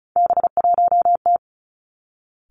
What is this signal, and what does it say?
Morse code, heard as a keyed tone, about 36 wpm, 700 Hz. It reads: B1T
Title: Three Challenging Letter-Number Combinations 35wpm